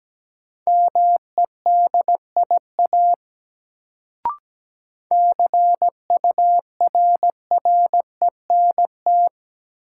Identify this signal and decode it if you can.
Morse code, heard as a keyed tone, about 17 wpm, 700 Hz. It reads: MEDIA CURRENT
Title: Sentences from Top 1000 Words - Review Part 4 - New Word Drill - 17wpm